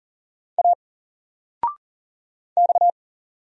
Morse code, heard as a keyed tone, about 40 wpm, 700 Hz. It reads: A X
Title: Single Letter-Number Alt 40wpm